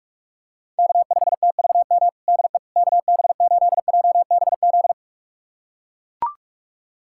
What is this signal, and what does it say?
Morse code, heard as a keyed tone, about 41 wpm, 700 Hz. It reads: K5TVM DE KB8JBZ
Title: USA Callsign DE Callsign 45wpm